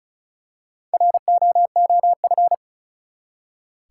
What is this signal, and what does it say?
Morse code, heard as a keyed tone, about 34 wpm, 700 Hz. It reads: ROOF
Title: Four Letter Words - 35wpm